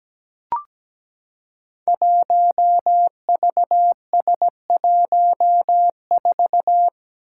Morse code, heard as a keyed tone, about 17 wpm, 700 Hz. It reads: 1VS14
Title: Five Challenging Letter-Number Combinations 17wpm